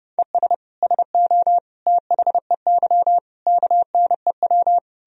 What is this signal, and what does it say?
Morse code, heard as a keyed tone, about 32 wpm, 700 Hz. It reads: ES SO THEY KNEW
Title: Sentences from Top 400 Words and Abbreviations 30wpm